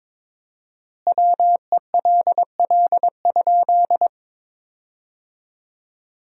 Morse code, heard as a keyed tone, about 22 wpm, 700 Hz. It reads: WELL?
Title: 4x Word-Spacing - Sentences from Top 200 Words - 22wpm